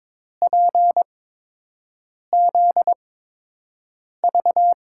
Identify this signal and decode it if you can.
Morse code, heard as a keyed tone, about 22 wpm, spaced as for 6 wpm, 700 Hz. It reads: PZV
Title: USA Call Signs - 5 Characters - 8x Character Spacing - 22wpm